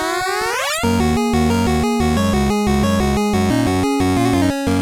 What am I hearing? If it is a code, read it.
There is no code.